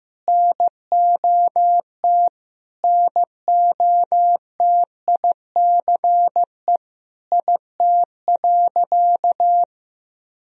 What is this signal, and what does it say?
Morse code, heard as a keyed tone, about 15 wpm, 700 Hz. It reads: NOT NOTICE IT.
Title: Sentences from Top 1000 Words - Review Part 2 - 15wpm